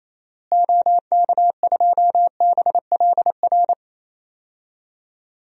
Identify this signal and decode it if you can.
Morse code, heard as a keyed tone, about 27 wpm, 700 Hz. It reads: OK2BLR
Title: DXCC-100 Call Signs - 6 Characters - Repeated 3x - 28wpm